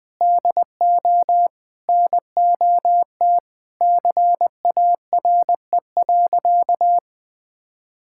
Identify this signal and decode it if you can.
Morse code, heard as a keyed tone, about 19 wpm, 700 Hz. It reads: DO NOT CARE.
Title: Sentences with 4-Word Phrases - Spoken Beginning - 20wpm